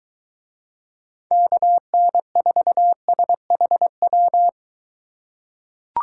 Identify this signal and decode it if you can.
Morse code, heard as a keyed tone, about 24 wpm, 700 Hz. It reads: KN4SHW